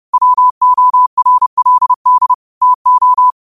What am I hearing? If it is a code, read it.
Morse code, heard as a keyed tone, about 30 wpm, 1000 Hz. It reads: WORLD TO
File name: qotdfiles / 30wpm